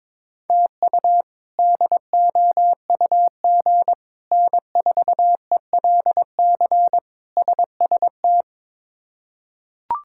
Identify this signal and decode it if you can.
Morse code, heard as a keyed tone, about 22 wpm, 700 Hz. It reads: TU DOUG N4ELC SST
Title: K1USN SST QSO Practice - 22wpm